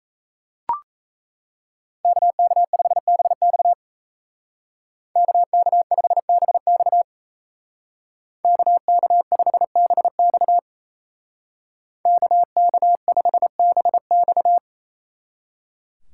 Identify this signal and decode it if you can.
Morse code, heard as a keyed tone, about 37 wpm, 700 Hz. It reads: KK5BX KK5BX KK5BX KK5BX